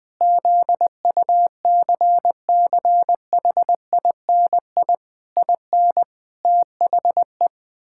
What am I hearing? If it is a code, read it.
Morse code, heard as a keyed tone, about 19 wpm, 700 Hz. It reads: ZUCCHINI IN THE
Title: Pangram Challenge v2 - 20wpm